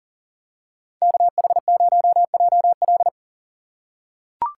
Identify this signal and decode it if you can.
Morse code, heard as a keyed tone, about 39 wpm, 690 Hz. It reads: KH0JL
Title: USA Call Signs - 5 Characters - 3x Character Spacing - 40wpm